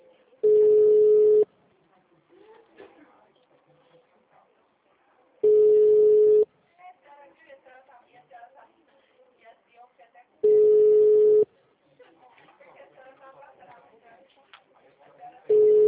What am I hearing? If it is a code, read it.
Ringback tone (Europe, 425 Hz), with speech.
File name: out- -2003- - - .69301